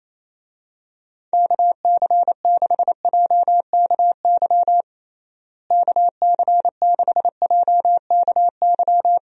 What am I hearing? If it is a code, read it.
Morse code, heard as a keyed tone, about 28 wpm, 700 Hz. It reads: KC6JKY KC6JKY
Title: USA Call Signs - 6 Characters - Repeated 2x - 28wpm